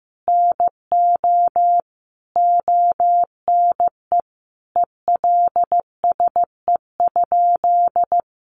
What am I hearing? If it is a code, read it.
Morse code, heard as a keyed tone, about 15 wpm, 700 Hz. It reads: NO ONE ELSE?
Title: Sentences from Top 800 Words 15wpm